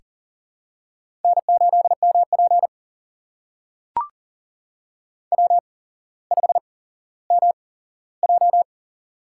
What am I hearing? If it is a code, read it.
Morse code, heard as a keyed tone, about 39 wpm, spaced as for 17 wpm, 700 Hz. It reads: N8MP W5MJ
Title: USA Call Signs - 4 Characters - 8x Character Spacing - 40wpm